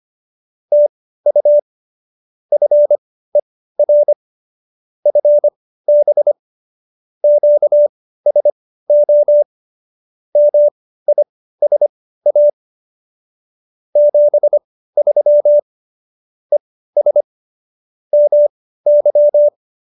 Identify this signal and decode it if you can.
Morse code, heard as a keyed tone, about 25 wpm, spaced as for 15 wpm, 590 Hz. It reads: TU FER FB QSO MISA 73 ES MY